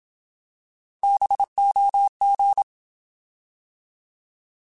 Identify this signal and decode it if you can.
Morse code, heard as a keyed tone, about 27 wpm, 800 Hz. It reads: BOG
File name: Alphabetical / Alphabetical mp3s / Bs @25